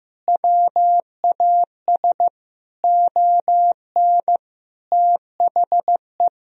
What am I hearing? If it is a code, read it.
Morse code, heard as a keyed tone, about 15 wpm, 700 Hz. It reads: WAS ON THE
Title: Sentences from Top 500 Words and Abbreviations 15wpm